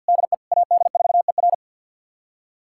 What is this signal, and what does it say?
Morse code, heard as a keyed tone, about 46 wpm, 700 Hz. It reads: DE AD4ER
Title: USA Callsign DE Callsign - 5 Characters - 50wpm